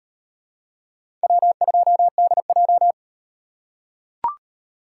Morse code, heard as a keyed tone, about 38 wpm, 700 Hz. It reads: W2DJ